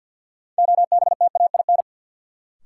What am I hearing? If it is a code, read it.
Morse code, heard as a keyed tone, about 50 wpm, 700 Hz. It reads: OBTAIN
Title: Sentences from Top 1300 Words - New Word Drill Alt - 50wpm